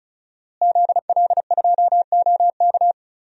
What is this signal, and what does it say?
Morse code, heard as a keyed tone, about 34 wpm, 700 Hz. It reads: ZL2OK
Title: DXCC-100 Call Signs - 5 Characters - Repeated 3x - 35wpm